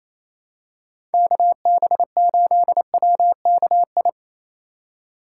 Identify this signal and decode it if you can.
Morse code, heard as a keyed tone, about 28 wpm, 700 Hz. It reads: KB8WKI
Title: USA Call Signs - 3x Character Spacing - 28wpm